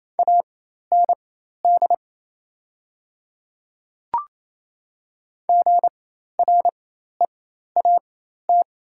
Morse code, heard as a keyed tone, about 28 wpm, spaced as for 13 wpm, 690 Hz. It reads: AND GREAT
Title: 4x Character Spacing - Top 200 Words - 28wpm